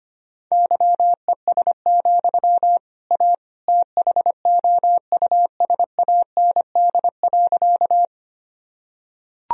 Morse code, heard as a keyed tone, about 25 wpm, 700 Hz. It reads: YES, A THOUSAND.
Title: Sentences from Top 1000 Words - Review Part 3 - 25wpm